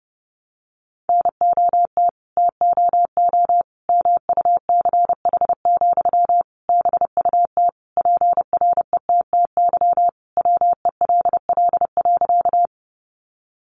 Morse code, heard as a keyed tone, about 30 wpm, 700 Hz. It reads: NOT TOO MUCH, BUT PRETTY WELL.
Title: Sentences from Top 1000 Words 30wpm